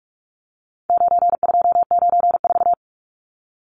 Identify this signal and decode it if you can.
Morse code, heard as a keyed tone, about 44 wpm, 700 Hz. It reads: 9294
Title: Numbers - 4-Digit 45wpm